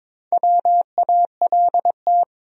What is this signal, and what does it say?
Morse code, heard as a keyed tone, about 22 wpm, 700 Hz. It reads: WALT
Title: USA Common Names - Max 4 Letters - 22wpm